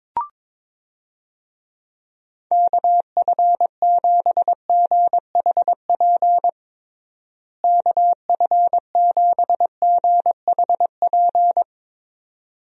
Morse code, heard as a keyed tone, about 22 wpm, 700 Hz. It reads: KF7GHP KF7GHP
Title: USA Call Signs - 6 Characters - Repeated 2x - 22wpm